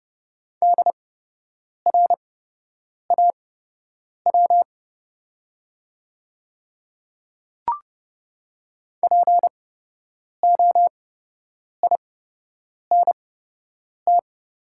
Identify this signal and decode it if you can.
Morse code, heard as a keyed tone, about 30 wpm, spaced as for 8 wpm, 700 Hz. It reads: DRAW POINT
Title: 8x Character Spacing - Top 500 Words - 30wpm